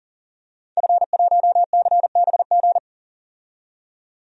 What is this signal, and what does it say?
Morse code, heard as a keyed tone, about 40 wpm, 700 Hz. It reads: F1CBG
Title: DXCC-100 Call Signs - 5 Characters - 40wpm